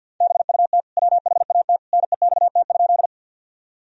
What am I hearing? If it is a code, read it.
Morse code, heard as a keyed tone, about 49 wpm, 700 Hz. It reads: BUT WHAT NEXT?
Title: Sentences from Top 1000 Words - Review Part 1 - 50wpm